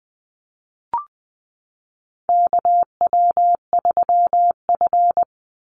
Morse code, heard as a keyed tone, about 20 wpm, 700 Hz. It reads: KW3F
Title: Call Signs - 20wpm